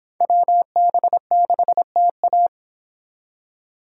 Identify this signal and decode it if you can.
Morse code, heard as a keyed tone, about 30 wpm, 700 Hz. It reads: WB6TA